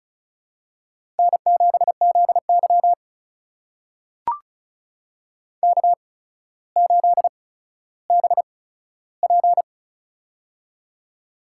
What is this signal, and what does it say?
Morse code, heard as a keyed tone, about 34 wpm, spaced as for 15 wpm, 700 Hz. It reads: N7ZY K8BP
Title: USA Call Signs - 8x Character Spacing - 35wpm